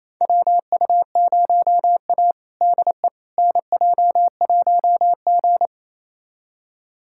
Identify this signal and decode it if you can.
Morse code, heard as a keyed tone, about 28 wpm, 700 Hz. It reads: WU0A DE NJ1G
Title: USA Callsign DE Callsign - 4 Characters - 28wpm